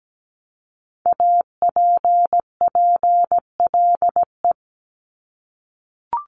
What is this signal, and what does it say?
Morse code, heard as a keyed tone, about 17 wpm, 700 Hz. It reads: APPLE
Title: Multi-letter Phonemes Repeated 3x 17wpm